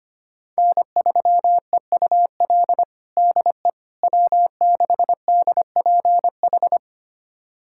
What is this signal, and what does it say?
Morse code, heard as a keyed tone, about 25 wpm, 700 Hz. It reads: N3EUL DE W6DPH
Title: USA Callsign DE Callsign - 5 Characters - 25wpm